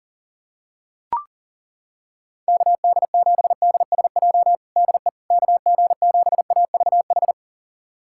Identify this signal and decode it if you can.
Morse code, heard as a keyed tone, about 40 wpm, 700 Hz. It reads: KD7DSJ DE KG7AVH
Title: USA Callsign DE Callsign 40wpm